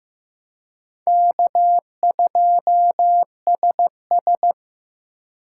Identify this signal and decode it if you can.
Morse code, heard as a keyed tone, about 18 wpm, 700 Hz. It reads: K2SS
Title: Speed Racing - USA 1x2 Call Signs 15wpm